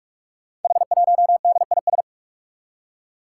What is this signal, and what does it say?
Morse code, heard as a keyed tone, about 44 wpm, 700 Hz. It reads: H1DIS
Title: Five Challenging Letter-Number Combinations 45wpm